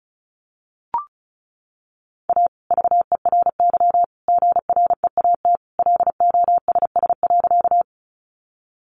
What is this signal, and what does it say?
Morse code, heard as a keyed tone, about 34 wpm, 700 Hz. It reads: A VERY GREAT LOSS.
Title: Sentences from Top 1400 Words 35wpm